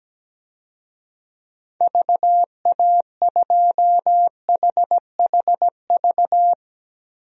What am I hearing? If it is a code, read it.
Morse code, heard as a keyed tone, about 17 wpm, 700 Hz. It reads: VA2HHV
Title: DXCC-100 Call Signs - 6 Characters - 17wpm